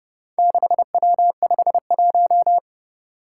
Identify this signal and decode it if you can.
Morse code, heard as a keyed tone, about 30 wpm, 700 Hz. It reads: 6W51
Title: Four Challenging Letter-Number Combinations 30wpm